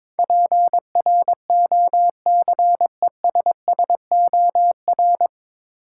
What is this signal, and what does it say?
Morse code, heard as a keyed tone, about 22 wpm, 700 Hz. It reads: PROCESSOR
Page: Morse Code Ninja practice 22wpm